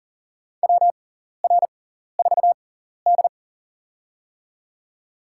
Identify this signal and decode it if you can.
Morse code, heard as a keyed tone, about 40 wpm, spaced as for 14 wpm, 700 Hz. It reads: WR4D
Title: USA Call Signs - 4 Characters - 6x Character Spacing - 40wpm